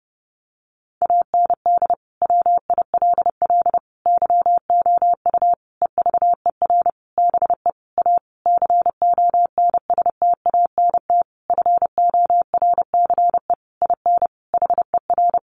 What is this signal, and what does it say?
Morse code, heard as a keyed tone, about 30 wpm, 700 Hz. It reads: AND WILL YOU EVER BE A CONSTANT FORCE IN HER
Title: Sentences from Top 1500 Words 30wpm